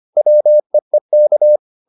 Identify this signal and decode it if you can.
Morse code, heard as a keyed tone, about 25 wpm, 600 Hz. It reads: WEEK